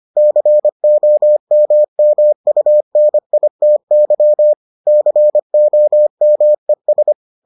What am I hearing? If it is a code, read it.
Morse code, heard as a keyed tone, about 25 wpm, 600 Hz. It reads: COMMUNITY COMES